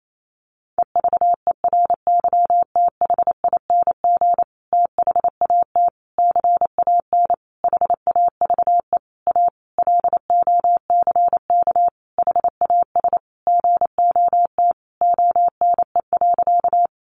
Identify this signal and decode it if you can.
Morse code, heard as a keyed tone, about 28 wpm, 700 Hz. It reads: EVERYTHING THAT CAN HAVE A LOCK HAS GOT ONE.
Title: Sentences from Top 1300 Words 28wpm